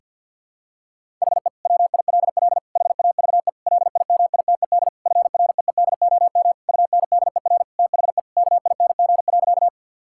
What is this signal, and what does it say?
Morse code, heard as a keyed tone, about 52 wpm, 700 Hz. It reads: HE WILL HAVE LIMITED FREEDOM UNDER THE KING.
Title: Sentences from Top 1000 Words - Review Part 6 - 50wpm